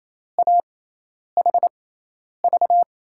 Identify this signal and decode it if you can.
Morse code, heard as a keyed tone, about 27 wpm, spaced as for 10 wpm, 700 Hz. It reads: AHV
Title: USA Call Signs - 6 Characters - 6x Character Spacing - 28wpm